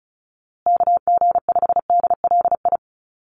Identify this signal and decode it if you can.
Morse code, heard as a keyed tone, about 34 wpm, 700 Hz. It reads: KG5DLI
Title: USA Call Signs Repeated 3x 35wpm